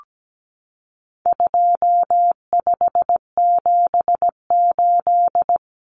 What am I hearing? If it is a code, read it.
Morse code, heard as a keyed tone, about 17 wpm, 700 Hz. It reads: 2578
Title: Common Cut Numbers - 4-Digit 17wpm